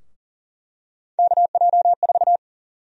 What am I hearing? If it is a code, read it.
Morse code, heard as a keyed tone, about 40 wpm, 700 Hz. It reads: KJ4
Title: USA Call Sign Prefixes - Repeated 2x - 40wpm